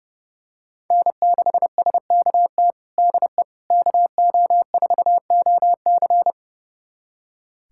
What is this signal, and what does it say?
Morse code, heard as a keyed tone, about 30 wpm, 700 Hz. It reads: N6SKT DE KO4OC
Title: USA Callsign DE Callsign - 5 Characters - 30wpm